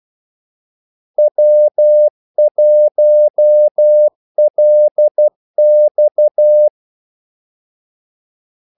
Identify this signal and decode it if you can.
Morse code, heard as a keyed tone, about 12 wpm, 600 Hz. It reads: W1LX
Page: 2 ekstra ordmellemrum mellem kaldesignalerne